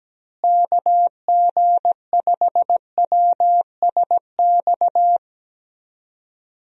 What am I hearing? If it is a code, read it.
Morse code, heard as a keyed tone, about 17 wpm, 700 Hz. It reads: KG5WSX
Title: USA Call Signs - 6 Characters - Repeated 3x - 17wpm